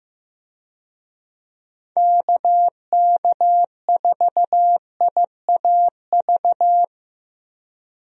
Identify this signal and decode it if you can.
Morse code, heard as a keyed tone, about 15 wpm, 700 Hz. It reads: KK4IAV
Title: USA Call Signs - 6 Characters - 15wpm